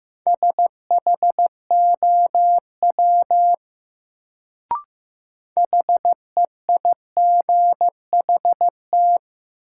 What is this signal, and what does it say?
Morse code, heard as a keyed tone, about 15 wpm, 700 Hz. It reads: SHOW HEIGHT
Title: Sets of 5 Words - New Word Drill - 15wpm